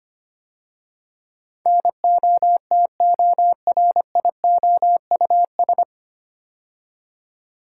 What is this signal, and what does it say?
Morse code, heard as a keyed tone, about 25 wpm, 700 Hz. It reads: NOTORIOUS
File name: practice / 25wpm / 9 Letter Words - First 3 Letters Given - 25wpm